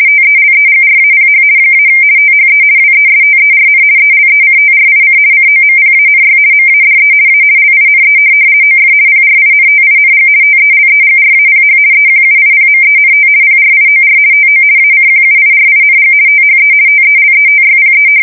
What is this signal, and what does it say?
RTTY, about 50 baud. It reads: TELEGRAPH - THE WESTERN ELECTRIC COMPANY MADE A LINE OF PRINTING TELEGRAPH EQUIPMENT PRIOR TO ACQUIRING THE TELETYPE
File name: fsk170-50B/teletype-corp